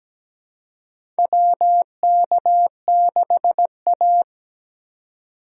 Morse code, heard as a keyed tone, about 17 wpm, 700 Hz. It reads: WK6A
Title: USA Call Signs - 4 Characters - 3x Character Spacing - 17wpm